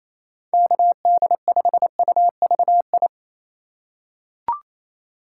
Morse code, heard as a keyed tone, about 28 wpm, 700 Hz. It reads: KD5UVI
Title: USA Call Signs - 6 Characters - 3x Character Spacing - 28wpm